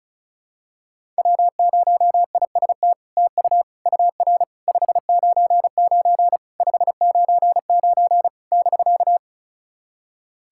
Morse code, heard as a keyed tone, about 36 wpm, 700 Hz. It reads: W0IST TU UR 599 599 <BK>
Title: SOTA QSO Practice 35wpm